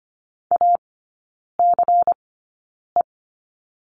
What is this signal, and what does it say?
Morse code, heard as a keyed tone, about 25 wpm, spaced as for 9 wpm, 700 Hz. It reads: ACE
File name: course / 25:10wpm / 093 Lesson 16e - Letter c - Three Characters 25wpm